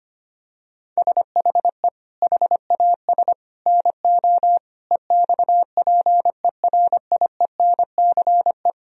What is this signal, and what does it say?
Morse code, heard as a keyed tone, about 25 wpm, 700 Hz. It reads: SHE HAS NO EXPERIENCE
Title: Sentences from Top 1000 Words - Review Part 4 - 25wpm